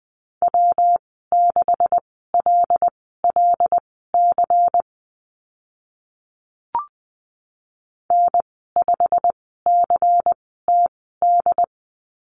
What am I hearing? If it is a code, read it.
Morse code, heard as a keyed tone, about 20 wpm, spaced as for 14 wpm, 700 Hz. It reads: W6LLC N5CTD
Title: 154 Lesson 25g - Letter p - Call Signs 20wpm